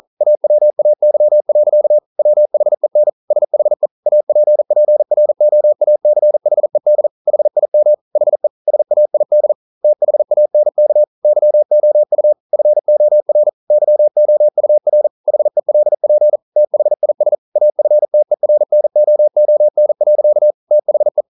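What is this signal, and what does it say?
Morse code, heard as a keyed tone, about 41 wpm, 600 Hz. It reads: AWAY. WHEN SHE APPROACHED HIM HE SAID THANK YOU FOR YOUR HELP THIS AFTERNOON. THE
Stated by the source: Morse code practice at 41 wpm, from CW med Gnister.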